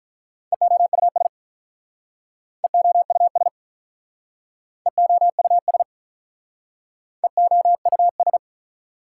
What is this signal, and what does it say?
Morse code, heard as a keyed tone, about 45 wpm, 700 Hz. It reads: EOUS EOUS EOUS EOUS